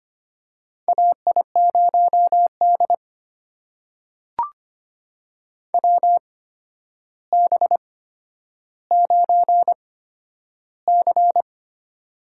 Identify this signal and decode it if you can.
Morse code, heard as a keyed tone, about 25 wpm, spaced as for 11 wpm, 700 Hz. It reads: AI0D WB9C
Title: USA Call Signs - 4 Characters - 8x Character Spacing - 25wpm